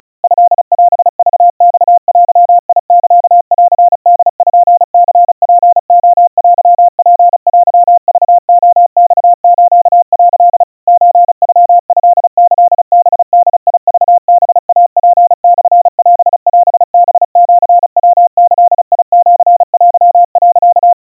Morse code (Japanese Wabun code), heard as a keyed tone, about 36 wpm, 700 Hz. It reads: トオクマデサンポニツレテツテクレマス」 ソノトキハボクハイツモオオハシヤギシテ、
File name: kawa / Morse sample / kamyu 130 cpm 700 Hz Crispy